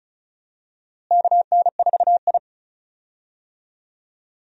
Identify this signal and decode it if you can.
Morse code, heard as a keyed tone, about 34 wpm, 700 Hz. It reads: KN4I
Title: USA Call Signs - 4 Characters - Repeated 3x - 35wpm